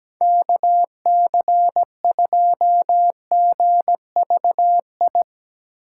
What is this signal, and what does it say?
Morse code, heard as a keyed tone, about 19 wpm, 700 Hz. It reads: KC2GVI
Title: Speed Racing - USA 2x3 Call Signs 17wpm